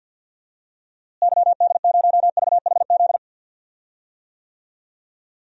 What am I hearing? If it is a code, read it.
Morse code, heard as a keyed tone, about 51 wpm, 700 Hz. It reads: YD0VHZ
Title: DXCC-100 Call Signs - 6 Characters - Repeated 2x - 50wpm